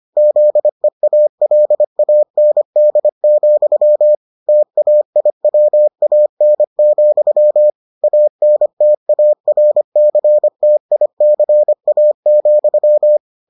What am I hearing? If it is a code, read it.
Morse code, heard as a keyed tone, about 25 wpm, 600 Hz. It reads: ZEALAND, TAIWAN, ANTARCTICA,